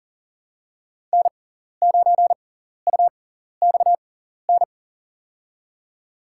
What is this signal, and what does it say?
Morse code, heard as a keyed tone, about 40 wpm, spaced as for 14 wpm, 700 Hz. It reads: N9UXN
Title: USA Call Signs - 5 Characters - 6x Character Spacing - 40wpm